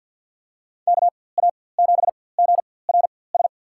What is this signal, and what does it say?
Morse code, heard as a keyed tone, about 48 wpm, spaced as for 23 wpm, 700 Hz. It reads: KA7GRS
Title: USA Call Signs - 4x Character Spacing - 50wpm